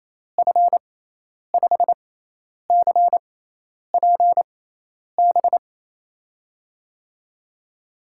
Morse code, heard as a keyed tone, about 28 wpm, spaced as for 10 wpm, 700 Hz. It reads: F5CPB
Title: USA Call Signs - 6x Character Spacing - 28wpm